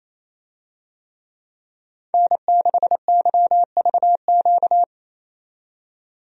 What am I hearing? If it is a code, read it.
Morse code, heard as a keyed tone, about 28 wpm, 700 Hz. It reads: N6YVQ
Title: USA Call Signs - 5 Characters - 28wpm